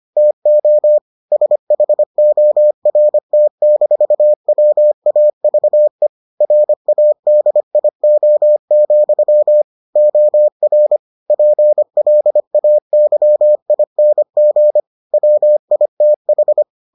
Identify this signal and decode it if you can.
Morse code, heard as a keyed tone, about 25 wpm, 600 Hz. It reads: TO SHORT-WAVE RADIO, OR PLAYING WITH